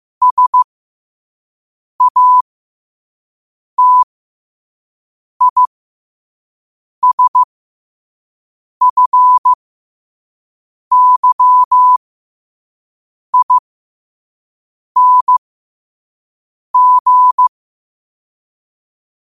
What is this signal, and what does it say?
Morse code, heard as a keyed tone, about 15 wpm, spaced as for 5 wpm, 1000 Hz. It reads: SATISFYING
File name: qotdfiles / 5wpm